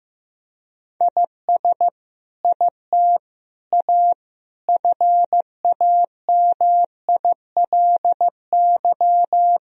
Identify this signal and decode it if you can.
Morse code, heard as a keyed tone, about 16 wpm, 700 Hz. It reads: IS IT A FAMILY
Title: Sentences from Top 1000 Words - Review Part 3 - 15wpm